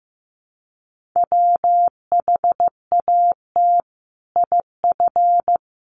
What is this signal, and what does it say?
Morse code, heard as a keyed tone, about 15 wpm, 700 Hz. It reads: WHAT IF
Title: Sentences from Top 1400 Words 15wpm